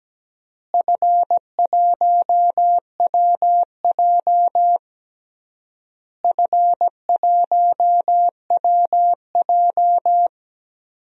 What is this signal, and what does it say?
Morse code, heard as a keyed tone, about 17 wpm, 700 Hz. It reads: F1WJ F1WJ
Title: DXCC-100 Call Signs - 4 Characters - Repeated 3x - 17wpm